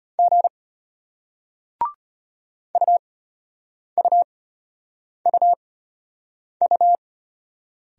Morse code, heard as a keyed tone, about 34 wpm, spaced as for 8 wpm, 700 Hz. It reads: G UUUU